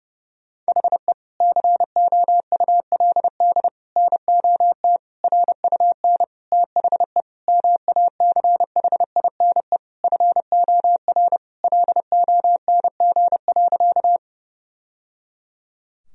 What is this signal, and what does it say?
Morse code, heard as a keyed tone, about 32 wpm, 700 Hz. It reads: HE COULD NOT RUN THE MACHINE FOR LONG.
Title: Sentences from Top 1000 Words - Review Part 2 - 30wpm